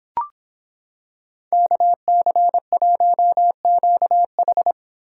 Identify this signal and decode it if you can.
Morse code, heard as a keyed tone, about 26 wpm, 700 Hz. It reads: KC1QH